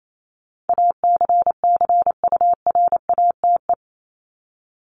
Morse code, heard as a keyed tone, about 28 wpm, 700 Hz. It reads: ACCURATE
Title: Top 1001-1250 Words 28wpm